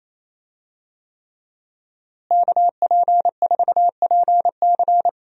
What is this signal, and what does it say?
Morse code, heard as a keyed tone, about 28 wpm, 700 Hz. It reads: KP4PC
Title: DXCC-100 Call Signs - 5 Characters - Repeated 3x - 28wpm